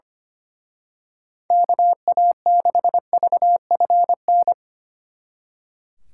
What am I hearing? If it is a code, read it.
Morse code, heard as a keyed tone, about 28 wpm, 700 Hz. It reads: KA6VFN
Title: Speed Racing - USA 2x3 Call Signs 25wpm